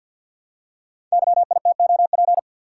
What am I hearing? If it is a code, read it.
Morse code, heard as a keyed tone, about 52 wpm, 700 Hz. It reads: YITOP
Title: Code Groups - 5 Letters - 50wpm